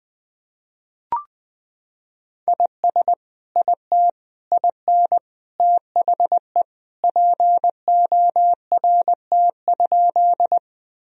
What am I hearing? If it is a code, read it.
Morse code, heard as a keyed tone, about 20 wpm, 700 Hz. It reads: IS IT IN THE PORT?
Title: Sentences from Top 1000 Words - Review Part 1 - 20wpm